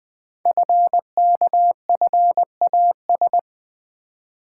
Morse code, heard as a keyed tone, about 20 wpm, 700 Hz. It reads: FKFAS
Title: Code Groups - 5 Letter:Numbers - 20wpm